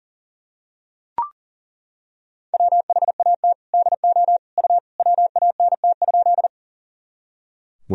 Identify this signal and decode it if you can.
Morse code, heard as a keyed tone, about 37 wpm, 700 Hz. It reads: WHAT DO U WANT?
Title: Sentences from Top 200 Words and Abbreviations 40wpm